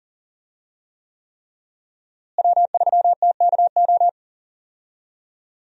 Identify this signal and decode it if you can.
Morse code, heard as a keyed tone, about 37 wpm, 700 Hz. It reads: W3TKO
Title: USA Call Signs - 5 Characters - Repeated 3x - 40wpm